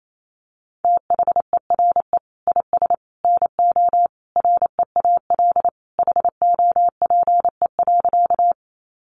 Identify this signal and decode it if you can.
Morse code, heard as a keyed tone, about 28 wpm, 700 Hz. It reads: THERE IS NO REAL HOPE.
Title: Sentences from Top 1000 Words 28wpm